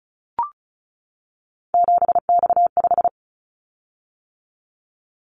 Morse code, heard as a keyed tone, about 34 wpm, 700 Hz. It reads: 7X5
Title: Callsign Prefixes for Top 100 DXCC - 35wpm